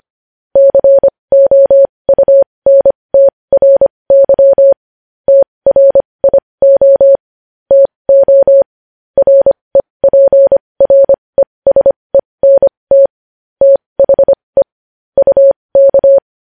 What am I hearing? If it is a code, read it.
Morse code, heard as a keyed tone, about 25 wpm, spaced as for 20 wpm, 550 Hz. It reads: COUNTRY TRIO TO REPRESENT THE UK
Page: news-headlines-in-morse-code-at-20-wpm